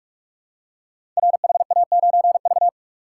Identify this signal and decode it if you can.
Morse code, heard as a keyed tone, about 44 wpm, 700 Hz. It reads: RHA9V
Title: Code Groups - 5 Letter:Numbers - 45wpm